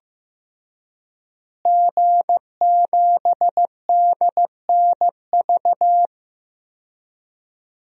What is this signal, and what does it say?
Morse code, heard as a keyed tone, about 15 wpm, 700 Hz. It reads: G7DNV
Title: DXCC-100 Call Signs - 5 Characters - Repeated 3x - 15wpm